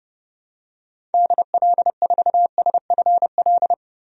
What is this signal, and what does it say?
Morse code, heard as a keyed tone, about 30 wpm, 700 Hz. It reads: DL4SFL
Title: DXCC-100 Call Signs - 6 Characters - 30wpm